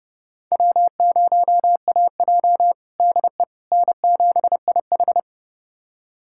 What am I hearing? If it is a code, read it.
Morse code, heard as a keyed tone, about 30 wpm, 700 Hz. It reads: W0AJ DE N7IH
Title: USA Callsign DE Callsign - 4 Characters - 30wpm